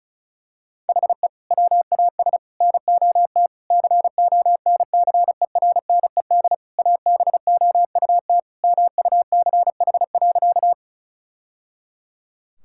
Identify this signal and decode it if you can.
Morse code, heard as a keyed tone, about 36 wpm, 700 Hz. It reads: HE WAS NOT CONCERNED ABOUT MUCH.
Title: Sentences from Top 1000 Words - Review Part 6 - 35wpm